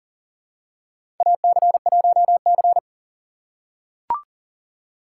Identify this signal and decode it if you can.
Morse code, heard as a keyed tone, about 43 wpm, 690 Hz. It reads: AC1C
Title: USA Call Signs - 4 Characters - 2x Character Spacing - 40wpm